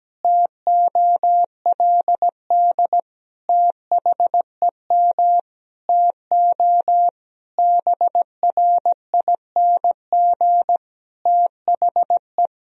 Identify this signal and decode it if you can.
Morse code, heard as a keyed tone, about 17 wpm, 700 Hz. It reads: TOLD THEM TO BRING THE
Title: Sentences from Top 500 Words and Abbreviations 17wpm